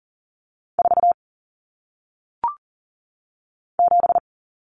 Morse code, heard as a keyed tone, about 38 wpm, 700 Hz. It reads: 4 7
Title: Numbers - 1-Digit 40wpm